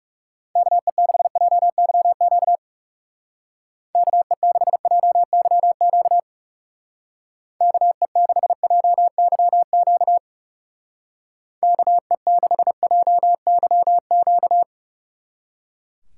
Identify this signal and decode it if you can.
Morse code, heard as a keyed tone, about 37 wpm, 700 Hz. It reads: KE6JYQ KE6JYQ KE6JYQ KE6JYQ
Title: Speed Racing - USA Call Signs 30wpm